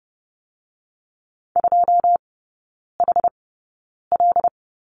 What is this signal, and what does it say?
Morse code, heard as a keyed tone, about 30 wpm, spaced as for 9 wpm, 700 Hz. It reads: 2HL
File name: course / 30:10wpm / 087 Lesson 15e - Number 5 - Three Characters 30wpm